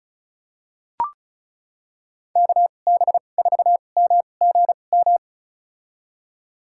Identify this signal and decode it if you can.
Morse code, heard as a keyed tone, about 36 wpm, spaced as for 25 wpm, 700 Hz. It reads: KB4MGM
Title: USA Call Signs - 2x Character Spacing - 35wpm